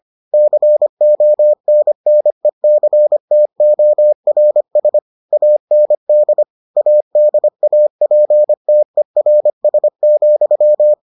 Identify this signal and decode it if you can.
Morse code, heard as a keyed tone, about 25 wpm, 600 Hz. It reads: CONNECTORS AND ADAPTERS,